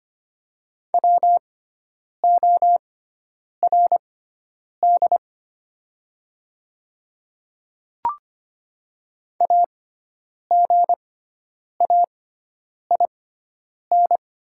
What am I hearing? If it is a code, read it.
Morse code, heard as a keyed tone, about 25 wpm, spaced as for 9 wpm, 700 Hz. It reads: WORD AGAIN
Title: 6x Character Spacing - Top 300 Words - 25wpm